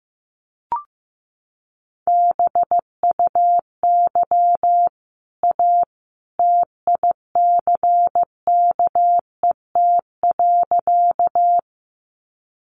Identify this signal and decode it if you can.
Morse code, heard as a keyed tone, about 15 wpm, 700 Hz. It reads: BUY A TICKET.
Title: Sentences from Top 1400 Words 15wpm